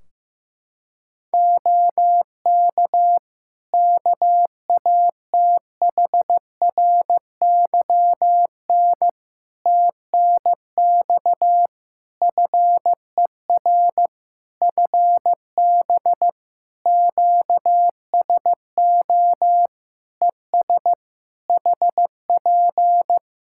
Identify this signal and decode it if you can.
Morse code, heard as a keyed tone, about 15 wpm, 700 Hz. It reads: OK KATHRYN TNX FER FB QSO ES HP
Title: K1USN SST QSO Practice 15wpm